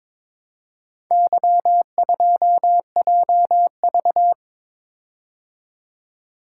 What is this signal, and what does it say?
Morse code, heard as a keyed tone, about 22 wpm, 700 Hz. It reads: Y2JV
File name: practice / 22wpm / Four Letter-Number Combinations - 22wpm